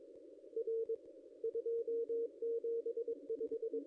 Morse code, heard as a keyed tone, about 22 wpm, 440 Hz. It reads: R 275
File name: ss-cw-k5tr-op-n5rz